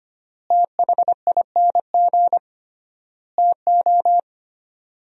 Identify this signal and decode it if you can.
Morse code, heard as a keyed tone, about 25 wpm, 700 Hz. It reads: THING TO
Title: 3x Word-Spacing - Sentences from Top 100 Words - 25wpm